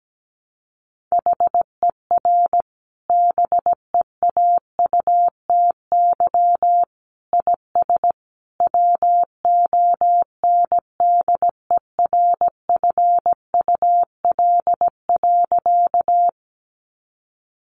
Morse code, heard as a keyed tone, about 17 wpm, 700 Hz. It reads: HER BEAUTY IS WONDERFUL.
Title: Sentences from Top 1100 Words 17wpm